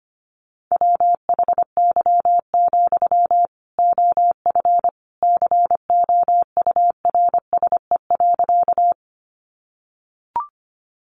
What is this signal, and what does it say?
Morse code, heard as a keyed tone, about 25 wpm, 700 Hz. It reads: WHY, OF COURSE.
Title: Sentences from Top 500 Words 25wpm